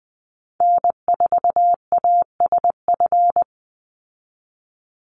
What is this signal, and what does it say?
Morse code, heard as a keyed tone, about 20 wpm, 700 Hz. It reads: N4ASF
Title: Call Signs - 20wpm